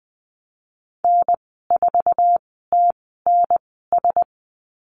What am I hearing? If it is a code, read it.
Morse code, heard as a keyed tone, about 20 wpm, spaced as for 14 wpm, 700 Hz. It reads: N4TNS
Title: 055 Lesson 10g - Letter r - Call Signs 20wpm